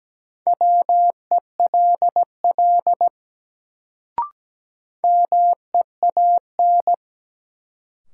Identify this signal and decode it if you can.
Morse code, heard as a keyed tone, about 17 wpm, 700 Hz. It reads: WELL MEAN
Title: Sentences from Top 200 Words - New Word Drill - 17wpm